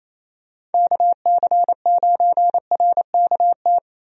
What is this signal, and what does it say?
Morse code, heard as a keyed tone, about 28 wpm, 700 Hz. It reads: KC9RKT
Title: USA Call Signs - 6 Characters - Repeated 2x - 28wpm